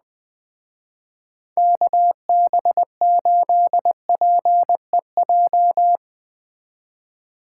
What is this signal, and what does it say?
Morse code, heard as a keyed tone, about 20 wpm, 700 Hz. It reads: KB8PEJ
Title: Speed Racing - USA 2x3 Call Signs 17wpm